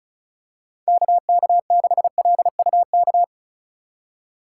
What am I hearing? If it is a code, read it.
Morse code, heard as a keyed tone, about 36 wpm, 700 Hz. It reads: KK6LUK
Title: USA Call Signs - 6 Characters - 4x Character Spacing - 35wpm